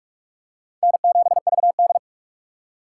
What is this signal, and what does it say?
Morse code, heard as a keyed tone, about 45 wpm, 700 Hz. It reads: N7VD
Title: USA Call Signs - 2x Character Spacing - 45wpm